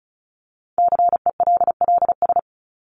Morse code, heard as a keyed tone, about 34 wpm, 700 Hz. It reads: CELLS
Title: ES:S Suffix - Top 100 Words with Base Words 35wpm